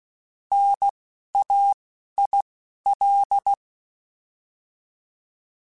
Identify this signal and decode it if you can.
Morse code, heard as a keyed tone, about 16 wpm, spaced as for 11 wpm, 800 Hz. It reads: NAIL